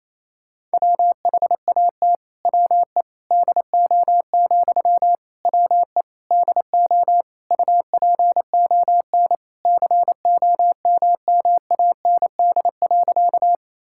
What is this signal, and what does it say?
Morse code, heard as a keyed tone, about 28 wpm, 700 Hz. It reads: WHAT WE DO, WE DO UPON COMMAND.
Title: Sentences from Top 1600 Words 28wpm